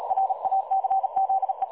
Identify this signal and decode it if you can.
Morse code, heard as a keyed tone, about 39 wpm, 770 Hz. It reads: RACZE
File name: dlazaawansowanych / nagraniazpasma / websdr recording 3565.0kHz